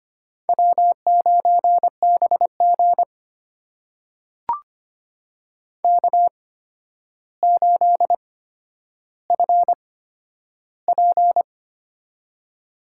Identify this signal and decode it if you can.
Morse code, heard as a keyed tone, about 25 wpm, spaced as for 11 wpm, 700 Hz. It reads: W9BG K8FP
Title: USA Call Signs - 4 Characters - 8x Character Spacing - 25wpm